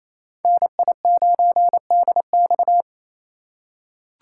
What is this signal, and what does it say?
Morse code, heard as a keyed tone, about 28 wpm, 700 Hz. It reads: NI9DX
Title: USA Call Signs - 5 Characters - 2x Character Spacing - 28wpm